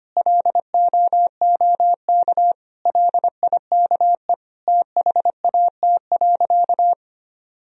Morse code, heard as a keyed tone, about 25 wpm, 700 Hz. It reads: LOOK LIKE THAT.
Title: Sentences from Top 1000 Words - Review Part 1 - 25wpm